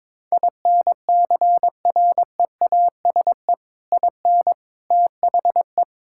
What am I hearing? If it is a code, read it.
Morse code, heard as a keyed tone, about 22 wpm, 700 Hz. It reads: INCREASE IN THE
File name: practice / 22wpm / Build A Phrase - 5-Word Phrases - 22wpm